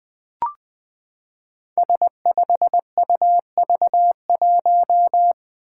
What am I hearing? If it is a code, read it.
Morse code, heard as a keyed tone, about 21 wpm, 700 Hz. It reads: S5UV1
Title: Five Challenging Letter-Number Combinations 20wpm